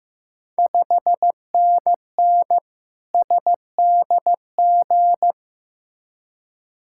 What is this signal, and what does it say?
Morse code, heard as a keyed tone, about 15 wpm, 700 Hz. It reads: 5NN SDG
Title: Ontario QSO Party Exchange 15wpm